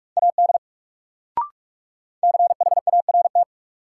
Morse code, heard as a keyed tone, about 46 wpm, 700 Hz. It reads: AD CHART
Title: Sentences from Top 1300 Words - New Word Drill - 45wpm